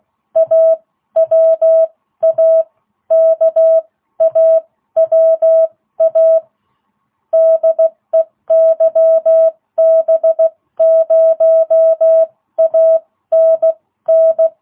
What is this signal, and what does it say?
Morse code, heard as a keyed tone, about 16 wpm, spaced as for 13 wpm, 660 Hz. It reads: AWA KAWA DE YB0ANN